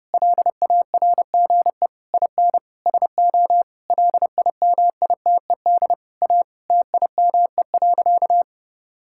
Morse code, heard as a keyed tone, about 30 wpm, 700 Hz. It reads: LARGE IN SO LIMITED A TIME.
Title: Sentences from Top 1000 Words - Review Part 6 - 30wpm